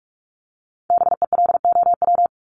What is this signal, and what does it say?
Morse code, heard as a keyed tone, about 46 wpm, 700 Hz. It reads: BELOW
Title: Top 501-750 Words 45wpm